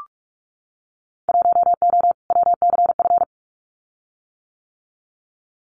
Morse code, heard as a keyed tone, about 44 wpm, 700 Hz. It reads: 1O WCF
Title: Winter Field Day Exchanges 45wpm